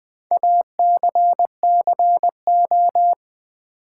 Morse code, heard as a keyed tone, about 20 wpm, 700 Hz. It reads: ACCO
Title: Build A Word - 4 Letters At A Time - 10-Letter Words - 20wpm